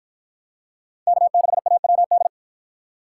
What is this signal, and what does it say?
Morse code, heard as a keyed tone, about 53 wpm, 700 Hz. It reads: K6AWD